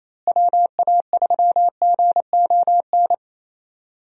Morse code, heard as a keyed tone, about 28 wpm, 700 Hz. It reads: WA3GON
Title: USA Call Signs - 6 Characters - Repeated 2x - 28wpm